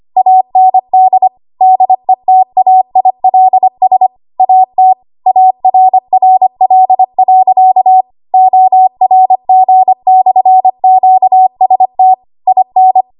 Morse code, heard as a keyed tone, about 25 wpm, 750 Hz. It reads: AND DETAILS AT ARRL.ORG/QST IN
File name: Archive / 25 WPM / 25WPM